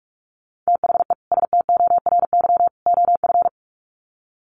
Morse code, heard as a keyed tone, about 44 wpm, 700 Hz. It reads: THE STORY OF
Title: Sets of 3 Words 45wpm